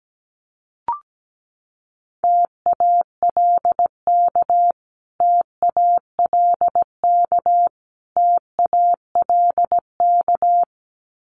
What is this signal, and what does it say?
Morse code, heard as a keyed tone, about 17 wpm, 700 Hz. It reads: TALK TALK TALK
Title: Top 500 Words Repeated 17wpm